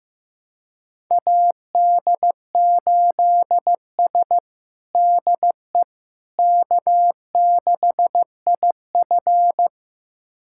Morse code, heard as a keyed tone, about 15 wpm, 700 Hz. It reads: AD8S DE K6IF
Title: USA Callsign DE Callsign - 4 Characters - 15wpm